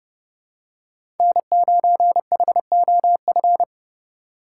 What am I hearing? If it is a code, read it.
Morse code, heard as a keyed tone, about 30 wpm, 700 Hz. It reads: N9HOF
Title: USA Call Signs - 5 Characters - 3x Character Spacing - 30wpm